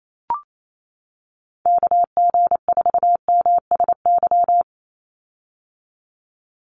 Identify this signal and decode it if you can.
Morse code, heard as a keyed tone, about 28 wpm, 700 Hz. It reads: KG4MSY
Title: USA Call Signs 28wpm